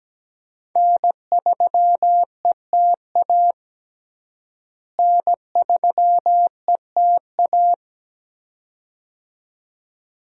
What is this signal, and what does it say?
Morse code, heard as a keyed tone, about 17 wpm, 700 Hz. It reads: N3ETA N3ETA
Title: USA Call Signs - 5 Characters - Repeated 2x - 17wpm